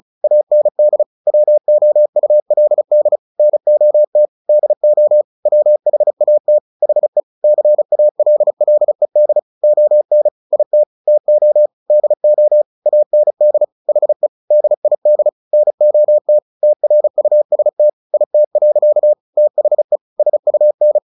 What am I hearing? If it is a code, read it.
Morse code, heard as a keyed tone, about 34 wpm, 600 Hz. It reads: AND WOULD NOT DO WHAT HE CALLED ON IT TO DO AND HE DID NOT TRUST IT. THE SUN
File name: the old man and the sea 2 35wpm